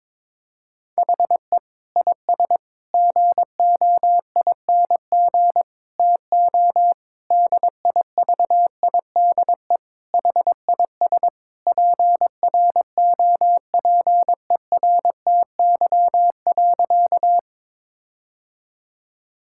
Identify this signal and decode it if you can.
Morse code, heard as a keyed tone, about 22 wpm, 700 Hz. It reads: HE IS GOING TO DIVIDE HIS PROPERTY.
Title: Sentences from Top 1600 Words 22wpm